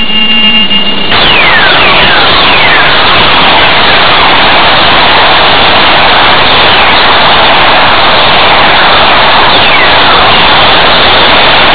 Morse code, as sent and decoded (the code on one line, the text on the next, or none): none